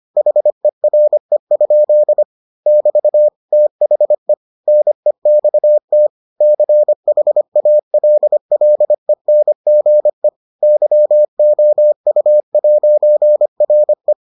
Morse code, heard as a keyed tone, about 25 wpm, 600 Hz. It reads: HERE? <BT> THE NEXT CHALLENGE YOU'RE